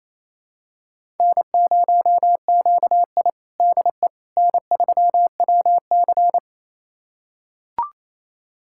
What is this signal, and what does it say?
Morse code, heard as a keyed tone, about 28 wpm, 700 Hz. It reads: N0QI DE N3WC
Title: USA Callsign DE Callsign - 4 Characters - 28wpm